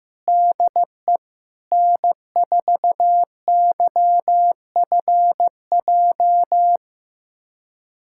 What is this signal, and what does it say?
Morse code, heard as a keyed tone, about 15 wpm, 700 Hz. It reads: DE N4YFJ
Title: USA Callsign DE Callsign 15wpm